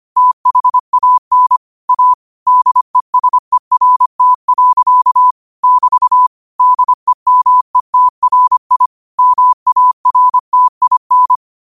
Morse code, heard as a keyed tone, about 25 wpm, 1000 Hz. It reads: THAN A DESERT. <BT> DEMETRI MARTIN
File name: qotdfiles / 25wpm